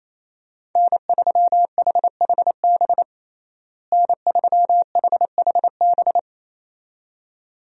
Morse code, heard as a keyed tone, about 27 wpm, 700 Hz. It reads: N3HHB N3HHB
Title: USA Call Signs - 5 Characters - Repeated 2x - 28wpm